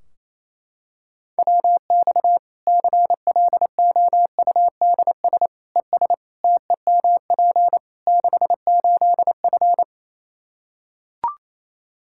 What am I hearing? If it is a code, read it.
Morse code, heard as a keyed tone, about 28 wpm, 700 Hz. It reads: WX CLOUDS ES TEMP 68F
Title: Practice QSO - Exchange 2 - Info - 28wpm